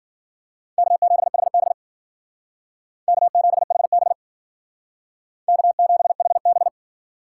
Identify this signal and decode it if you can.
Morse code, heard as a keyed tone, about 51 wpm, 700 Hz. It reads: K7HB K7HB K7HB